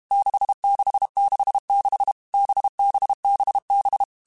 Morse code, heard as a keyed tone, about 32 wpm, 800 Hz. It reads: 6666 BBBB
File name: Groups@30 / Groups@30 mp3s / VB46@30